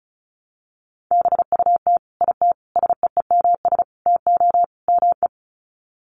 Morse code, heard as a keyed tone, about 34 wpm, 700 Hz. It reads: BUT IT SEEMS TO ME
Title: Sets of 5 Words Encore2 35wpm